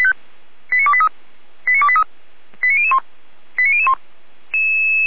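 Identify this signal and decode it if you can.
Five-tone selective call (ZVEI), heard as quick runs of five tones, tones of about 70 ms. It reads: 89283 89283 89001 89001
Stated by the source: ZVEI five-tone call 89273 received by SDR.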